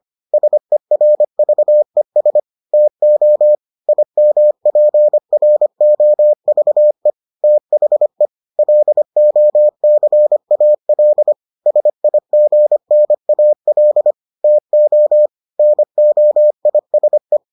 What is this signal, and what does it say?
Morse code, heard as a keyed tone, about 25 wpm, 600 Hz. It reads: SERVES TO IMPROVE THE LOCAL SIGNAL TO NOISE